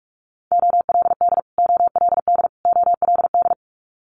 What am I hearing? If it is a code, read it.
Morse code, heard as a keyed tone, about 46 wpm, 700 Hz. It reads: OLD OLD OLD
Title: Multi-letter Phonemes Repeated 3x 45wpm